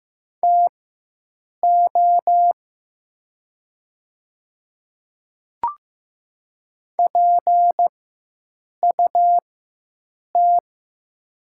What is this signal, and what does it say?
Morse code, heard as a keyed tone, about 15 wpm, spaced as for 7 wpm, 700 Hz. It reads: TO PUT
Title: 4x Character Spacing - Top 200 Words - 15wpm